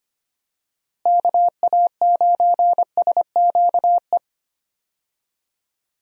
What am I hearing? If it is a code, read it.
Morse code, heard as a keyed tone, about 25 wpm, 700 Hz. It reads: KA9SQE
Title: USA Call Signs - 6 Characters - Repeated 2x - 25wpm